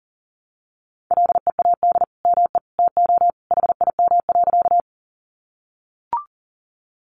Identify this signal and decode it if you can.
Morse code, heard as a keyed tone, about 40 wpm, 700 Hz. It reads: LEAD ME TO HIM.
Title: Sentences from Top 500 Words 40wpm